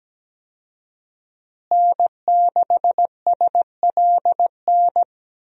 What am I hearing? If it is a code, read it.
Morse code, heard as a keyed tone, about 17 wpm, 700 Hz. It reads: N6SLN
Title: USA Call Signs - 5 Characters - Repeated 2x - 17wpm